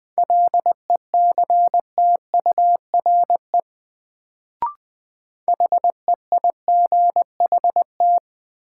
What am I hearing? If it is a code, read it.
Morse code, heard as a keyed tone, about 20 wpm, 700 Hz. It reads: LECTURE HEIGHT
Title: Sentences from Top 1600 Words - New Word Drill Alt - 20wpm